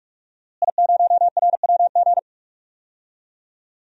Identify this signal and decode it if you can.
Morse code, heard as a keyed tone, about 47 wpm, 700 Hz. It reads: I0RWG
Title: DXCC-100 Call Signs - 5 Characters - Repeated 2x - 45wpm